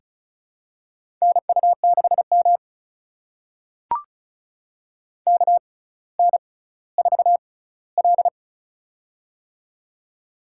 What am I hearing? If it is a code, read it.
Morse code, heard as a keyed tone, about 36 wpm, spaced as for 18 wpm, 700 Hz. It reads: NU6M KN4L
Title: USA Call Signs - 4 Characters - 6x Character Spacing - 35wpm